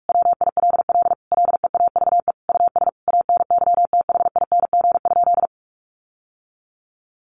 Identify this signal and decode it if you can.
Morse code, heard as a keyed tone, about 44 wpm, 700 Hz. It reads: WILL LEAVE US ANYTHING?
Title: Sentences from Top 1100 Words 45wpm